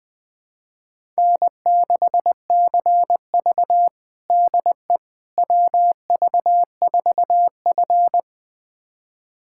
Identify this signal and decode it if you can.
Morse code, heard as a keyed tone, about 20 wpm, 700 Hz. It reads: N6CV DE WV4F
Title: USA Callsign DE Callsign - 4 Characters - 20wpm